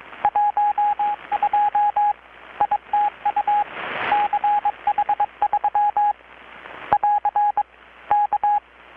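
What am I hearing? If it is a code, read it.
Morse code, heard as a keyed tone, about 22 wpm, 830 Hz, with radio noise. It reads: 12 ITU CH3 <AR> K
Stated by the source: Morse from KPH's Night of Nights XX maritime radiotelegraphy event, received in Chiapas, Mexico about 3,800 km away on a Kenwood R-600 with a grounded 10 m long-wire antenna.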